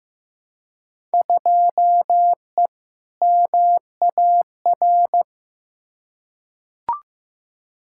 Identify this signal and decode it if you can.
Morse code, heard as a keyed tone, about 15 wpm, 700 Hz. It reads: 2E MAR
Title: Field Day Exchanges Encore - 15wpm